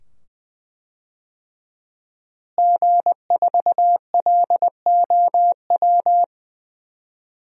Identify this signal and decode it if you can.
Morse code, heard as a keyed tone, about 20 wpm, 700 Hz. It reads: G4LOW
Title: DXCC-100 Call Signs - 5 Characters - Repeated 2x - 20wpm